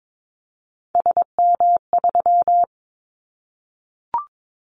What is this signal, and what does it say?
Morse code, heard as a keyed tone, about 22 wpm, 700 Hz. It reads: SM3
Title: Callsign Prefixes for Top 25 DXCC - 22wpm